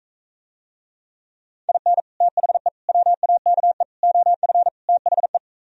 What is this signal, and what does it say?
Morse code, heard as a keyed tone, about 45 wpm, 700 Hz. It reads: IN THE WAKE OF THE